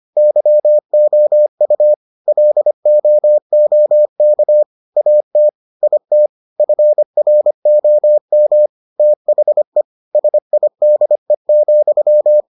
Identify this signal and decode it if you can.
Morse code, heard as a keyed tone, about 25 wpm, 600 Hz. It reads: YOU LOOK AT IT FROM THE SIDE,